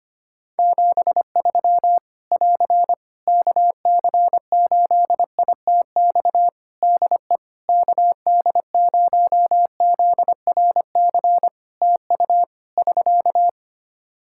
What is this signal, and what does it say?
Morse code, heard as a keyed tone, about 25 wpm, 700 Hz. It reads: 73 <AR> KC8ITX DE KD0ZRC TU <SK>
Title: Practice QSO - Exchange 4 - Ending - 25wpm